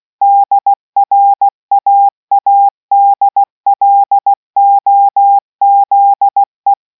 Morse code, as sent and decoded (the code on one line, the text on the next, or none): -.. .-. .- .- -.. .-.. --- --.. .
DRAADLOZE